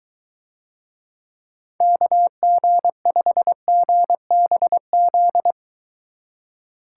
Morse code, heard as a keyed tone, about 21 wpm, 700 Hz. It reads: KG5GBZ